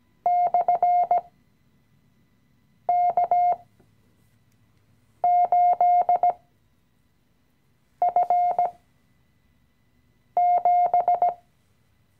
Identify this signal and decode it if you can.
Morse code, heard as a keyed tone, about 17 wpm, spaced as for 5 wpm, 710 Hz. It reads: /K8F7
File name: CW-Online-Kurs / Anfänger / Lektion 017